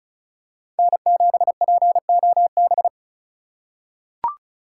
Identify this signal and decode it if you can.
Morse code, heard as a keyed tone, about 36 wpm, 700 Hz. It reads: N7POB
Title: USA Call Signs - 5 Characters - 2x Character Spacing - 35wpm